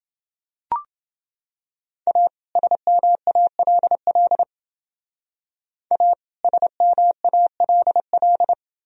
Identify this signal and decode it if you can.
Morse code, heard as a keyed tone, about 29 wpm, 700 Hz. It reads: A SMALL A SMALL